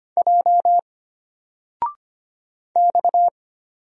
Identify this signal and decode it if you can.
Morse code, heard as a keyed tone, about 25 wpm, 700 Hz. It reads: J X
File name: Single-Letter-Number-25wpm